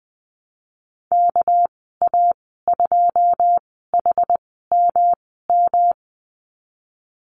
Morse code, heard as a keyed tone, about 20 wpm, spaced as for 15 wpm, 700 Hz. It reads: KA2HMM
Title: 238 Lesson 38g - Letter q - Call Signs 20wpm